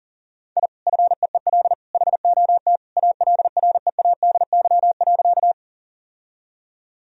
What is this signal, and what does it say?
Morse code, heard as a keyed tone, about 40 wpm, 700 Hz. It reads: I FEEL HOT ALREADY.
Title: Sentences from Top 1000 Words - Review Part 3 - 40wpm